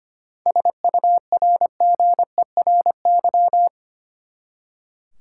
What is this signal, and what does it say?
Morse code, heard as a keyed tone, about 25 wpm, 700 Hz. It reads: SURGERY
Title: Seven Letter Words - 25wpm